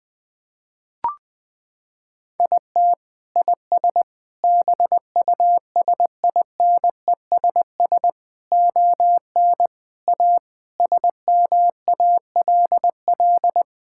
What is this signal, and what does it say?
Morse code, heard as a keyed tone, about 20 wpm, 700 Hz. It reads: IT IS BUSINESS ON A SMALL
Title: Sentences from Top 1000 Words - Review Part 5 - 20wpm